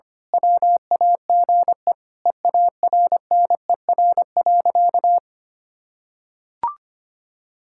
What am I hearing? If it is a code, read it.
Morse code, heard as a keyed tone, about 25 wpm, 700 Hz. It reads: WAGE EARNER.
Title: Pangram Challenge v2 - 25wpm